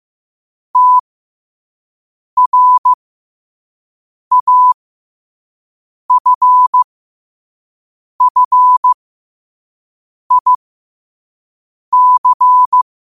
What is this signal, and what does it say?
Morse code, heard as a keyed tone, about 15 wpm, spaced as for 5 wpm, 1000 Hz. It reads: TRAFFIC
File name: qotdfiles / 5wpm